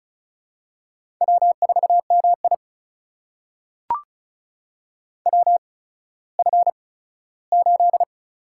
Morse code, heard as a keyed tone, about 36 wpm, spaced as for 17 wpm, 700 Hz. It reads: W4MI WF8
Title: USA Call Signs - 8x Character Spacing - 35wpm